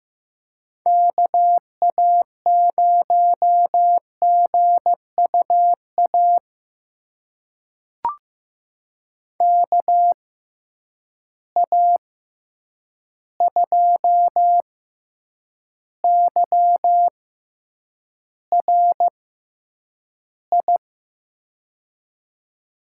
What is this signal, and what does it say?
Morse code, heard as a keyed tone, about 15 wpm, spaced as for 8 wpm, 700 Hz. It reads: KA0GUA KA2YRI
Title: USA Call Signs - 6 Characters - 6x Character Spacing - 15wpm